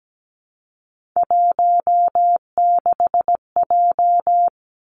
Morse code, heard as a keyed tone, about 17 wpm, 700 Hz. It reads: 16J
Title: Three Challenging Letter-Number Combinations 17wpm